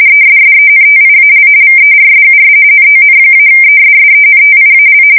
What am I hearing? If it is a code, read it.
RTTY, about 45.45 baud. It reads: ESSENTIAL VERSUS NON-ESSENTIAL
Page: 170 Shift 2125/2295